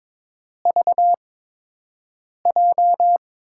Morse code, heard as a keyed tone, about 22 wpm, spaced as for 6 wpm, 700 Hz. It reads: VJ
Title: USA Call Signs - 5 Characters - 8x Character Spacing - 22wpm